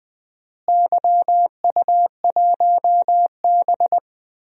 Morse code, heard as a keyed tone, about 20 wpm, 700 Hz. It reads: YU1B
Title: DXCC-100 Call Signs - 4 Characters - Repeated 3x - 20wpm